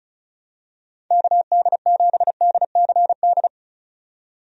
Morse code, heard as a keyed tone, about 34 wpm, 700 Hz. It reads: KD7DCD
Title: USA Call Signs - 6 Characters - 35wpm